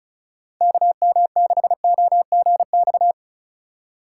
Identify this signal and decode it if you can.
Morse code, heard as a keyed tone, about 36 wpm, 700 Hz. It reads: KM6OGX
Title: USA Call Signs - 6 Characters - 3x Character Spacing - 35wpm